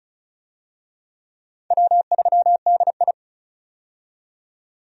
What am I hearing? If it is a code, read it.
Morse code, heard as a keyed tone, about 36 wpm, 700 Hz. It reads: W3DI
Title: USA Call Signs - 4 Characters - 35wpm